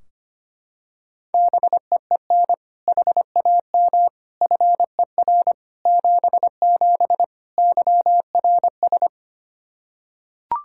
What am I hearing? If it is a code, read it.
Morse code, heard as a keyed tone, about 25 wpm, 700 Hz. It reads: BEEN HAM FER 77 YRS
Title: Practice QSO - Exchange 3 - More Info - 25wpm